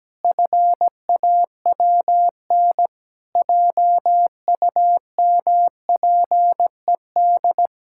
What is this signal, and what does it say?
Morse code, heard as a keyed tone, about 17 wpm, 700 Hz. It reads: FAWN JUMPED
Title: Pangram Challenge v2 - 17wpm